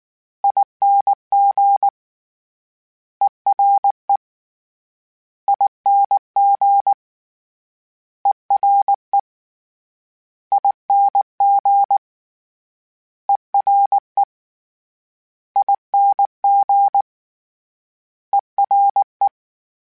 Morse code, heard as a keyed tone, about 19 wpm, 800 Hz. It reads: ING ERE ING ERE ING ERE ING ERE
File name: Special INGs / Special INGs-mp3s / Ere ing @18